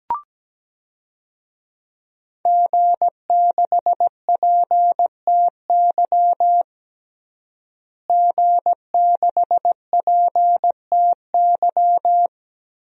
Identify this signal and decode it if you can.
Morse code, heard as a keyed tone, about 17 wpm, 700 Hz. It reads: G6PTY G6PTY
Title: DXCC-100 Call Signs - 5 Characters - Repeated 2x - 17wpm